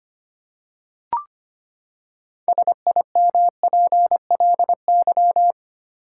Morse code, heard as a keyed tone, about 25 wpm, 700 Hz. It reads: SIMPLY
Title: Six Letter Words - 25wpm